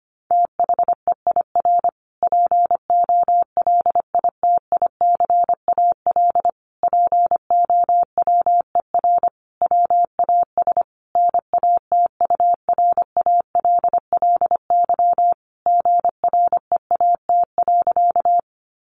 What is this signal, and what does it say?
Morse code, heard as a keyed tone, about 25 wpm, 700 Hz. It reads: THEIR POLITICAL POWER WAS NATURALLY GREAT.
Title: Sentences from Top 1300 Words 25wpm